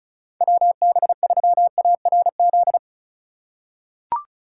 Morse code, heard as a keyed tone, about 36 wpm, 700 Hz. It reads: WB3ARZ
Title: USA Call Signs - 6 Characters - 2x Character Spacing - 35wpm